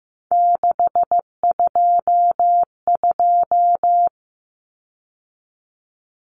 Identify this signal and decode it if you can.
Morse code, heard as a keyed tone, about 15 wpm, 700 Hz. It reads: 622
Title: Common Cut Numbers - 3-Digit 15wpm